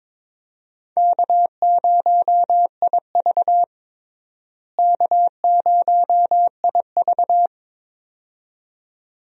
Morse code, heard as a keyed tone, about 22 wpm, 700 Hz. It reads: K0IV K0IV
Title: USA Call Signs - 4 Characters - Repeated 2x - 22wpm